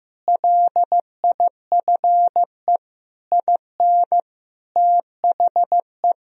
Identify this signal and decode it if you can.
Morse code, heard as a keyed tone, about 16 wpm, 700 Hz. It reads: LIFE IN THE
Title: Sentences from Top 1000 Words - Review Part 1 - 15wpm